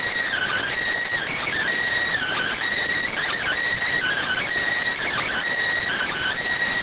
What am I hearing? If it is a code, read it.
Emergency Alert System (SAME) tones (header bursts), heard as harsh buzzing bursts. It is heard from 0.3 to 6.4 s.